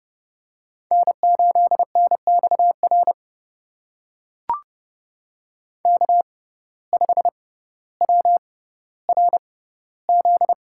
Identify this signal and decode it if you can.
Morse code, heard as a keyed tone, about 32 wpm, spaced as for 16 wpm, 700 Hz. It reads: N8NXR K5WRZ
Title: USA Call Signs - 5 Characters - 6x Character Spacing - 30wpm